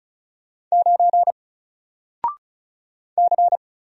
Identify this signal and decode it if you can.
Morse code, heard as a keyed tone, about 34 wpm, 700 Hz. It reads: 9 C
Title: Single Character (Letter:Number:Punctuation:Prosign) Alt 35wpm